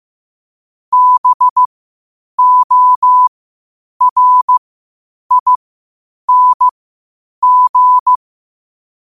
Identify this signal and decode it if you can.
Morse code, heard as a keyed tone, about 15 wpm, spaced as for 8 wpm, 1000 Hz. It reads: BORING
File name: qotdfiles / 8wpm